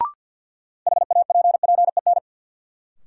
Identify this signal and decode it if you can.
Morse code, heard as a keyed tone, about 47 wpm, 700 Hz. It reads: HAPPEN
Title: Sentences from Top 400 Words - New Word Drill - 50wpm